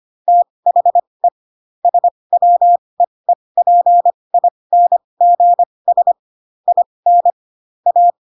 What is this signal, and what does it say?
Morse code, heard as a keyed tone, about 25 wpm, spaced as for 20 wpm, 700 Hz. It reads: THE SWEEPINGS IN A